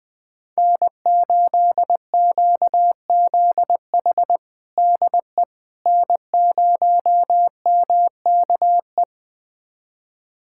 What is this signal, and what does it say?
Morse code, heard as a keyed tone, about 20 wpm, 700 Hz. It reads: N8QZH DE N0MKE
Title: USA Callsign DE Callsign - 5 Characters - 20wpm